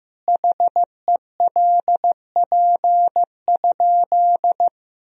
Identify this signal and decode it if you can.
Morse code, heard as a keyed tone, about 15 wpm, 700 Hz. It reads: HELP?
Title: Sentences from Top 200 Words and Abbreviations 15wpm